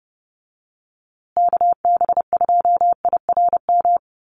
Morse code, heard as a keyed tone, about 30 wpm, 700 Hz. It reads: KB2IRM
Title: USA Call Signs Repeated 2x 30wpm